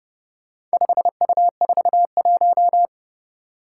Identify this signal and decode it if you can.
Morse code, heard as a keyed tone, about 30 wpm, 700 Hz. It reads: 5U41
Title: Four Challenging Letter-Number Combinations 30wpm